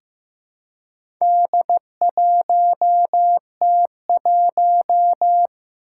Morse code, heard as a keyed tone, about 15 wpm, 700 Hz. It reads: D1T1
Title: Four Challenging Letter-Number Combinations 15wpm